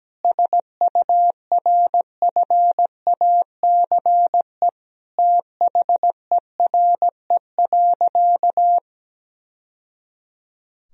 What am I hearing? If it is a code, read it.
Morse code, heard as a keyed tone, about 17 wpm, 700 Hz. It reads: SURFACE THERE.
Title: Sentences from Top 1000 Words - Review Part 3 - 17wpm